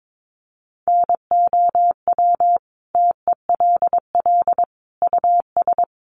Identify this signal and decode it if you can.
Morse code, heard as a keyed tone, about 22 wpm, 700 Hz. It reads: NOW TELL US
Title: Sentences from Top 600 Words 22wpm